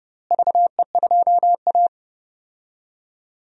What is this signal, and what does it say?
Morse code, heard as a keyed tone, about 31 wpm, 700 Hz. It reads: VE2A
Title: DXCC-100 Call Signs - 4 Characters - Repeated 3x - 30wpm